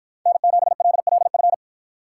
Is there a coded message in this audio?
Yes: Morse code from 0.3 to 1.5 s, heard as a keyed tone, about 51 wpm, 700 Hz. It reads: N7LLF